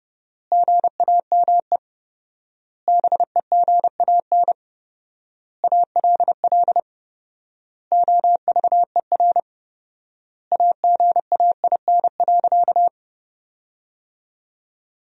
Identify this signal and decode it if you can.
Morse code, heard as a keyed tone, about 30 wpm, 700 Hz. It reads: GAME BEGAN ALL OVER AGAIN.
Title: 4x Word-Spacing - Sentences from Top 500 Words - 30wpm